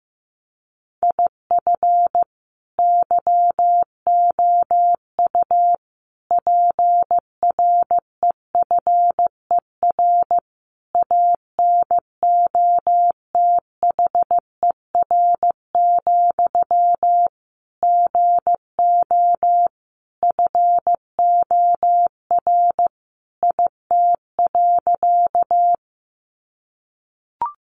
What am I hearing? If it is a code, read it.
Morse code, heard as a keyed tone, about 15 wpm, 700 Hz. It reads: IF YOU PREFER ANOTHER, GO FOR IT.
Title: Sentences from Top 1100 Words 15wpm